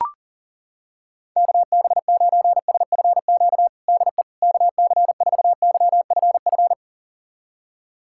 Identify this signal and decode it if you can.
Morse code, heard as a keyed tone, about 37 wpm, 700 Hz. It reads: KB9SFQ DE KC4YFF
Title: USA Callsign DE Callsign - 6 Characters - 40wpm